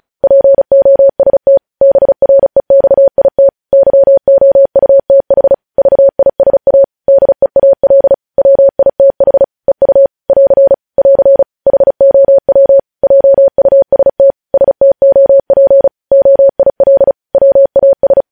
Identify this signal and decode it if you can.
Morse code, heard as a keyed tone, about 36 wpm, 550 Hz. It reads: POST BREXIT YOUTH VISA DEAL WITH EU <AR> <AR> HOW JUST STOP OIL WAS
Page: News Headlines in Morse Code at 35 WPM